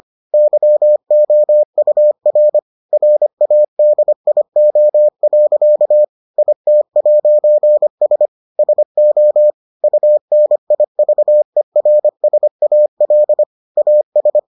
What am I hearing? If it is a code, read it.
Morse code, heard as a keyed tone, about 25 wpm, 600 Hz. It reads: YOUR RADIO. IT'S SO UNIVERSAL AS